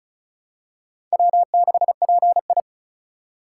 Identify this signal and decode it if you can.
Morse code, heard as a keyed tone, about 36 wpm, 700 Hz. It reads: W6PI
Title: USA Call Signs - 4 Characters - 2x Character Spacing - 35wpm